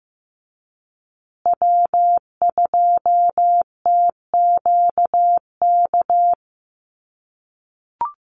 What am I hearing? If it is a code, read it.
Morse code, heard as a keyed tone, about 15 wpm, 700 Hz. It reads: W2TQK
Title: USA Call Signs Repeated 3x 15wpm